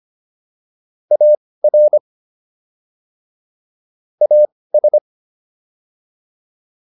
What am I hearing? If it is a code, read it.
Morse code, heard as a keyed tone, about 25 wpm, spaced as for 18 wpm, 600 Hz. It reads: AR AS